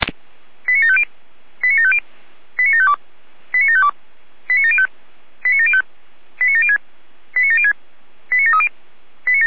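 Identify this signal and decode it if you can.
Five-tone selective call (ZVEI), heard as quick runs of five tones, tones of about 70 ms. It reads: 89750 89750 89752 89752 89705 89705 89706 89706 89730
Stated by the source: ZVEI five-tone selective call 89463, received by SDR.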